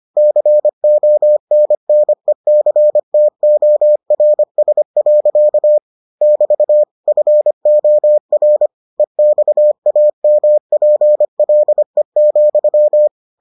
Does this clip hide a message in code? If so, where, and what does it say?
Yes: Morse code from 0.2 to 13.1 s, heard as a keyed tone, about 25 wpm, 600 Hz. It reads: CONNECTORS. <BT> FOR EXAMPLE,